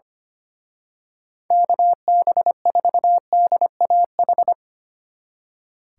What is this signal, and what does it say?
Morse code, heard as a keyed tone, about 28 wpm, 700 Hz. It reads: KB4DAH
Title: Speed Racing - USA 2x3 Call Signs 25wpm